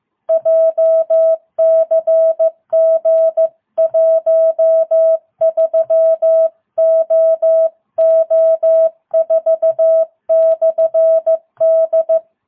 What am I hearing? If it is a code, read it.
Morse code, heard as a keyed tone, about 15 wpm, 660 Hz. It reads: JCG 13OO4/D